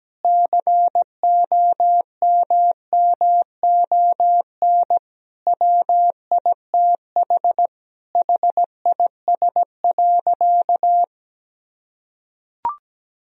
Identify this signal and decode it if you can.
Morse code, heard as a keyed tone, about 17 wpm, 700 Hz. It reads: COMMON WITH HIS.
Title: Sentences from Top 1000 Words - Review Part 3 - 17wpm